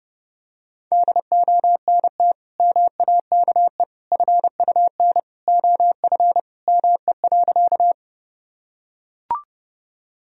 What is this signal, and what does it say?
Morse code, heard as a keyed tone, about 30 wpm, 700 Hz. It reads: DONT MAKE FUN OF ME.
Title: Sentences from Top 1000 Words - Review Part 4 - 30wpm